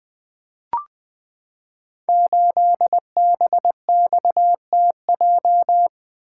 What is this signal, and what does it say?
Morse code, heard as a keyed tone, about 20 wpm, 700 Hz. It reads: 8BXTJ
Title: Code Groups - 5 Letter:Numbers - 20wpm